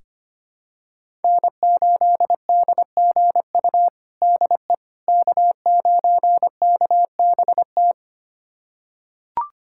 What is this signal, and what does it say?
Morse code, heard as a keyed tone, about 25 wpm, 700 Hz. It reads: N8DGU DE K9KBT
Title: USA Callsign DE Callsign - 5 Characters - 25wpm